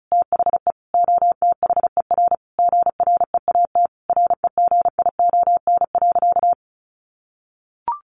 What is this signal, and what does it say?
Morse code, heard as a keyed tone, about 34 wpm, 700 Hz. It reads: THE OTHER GREAT REGION.
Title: Sentences from Top 1500 Words 35wpm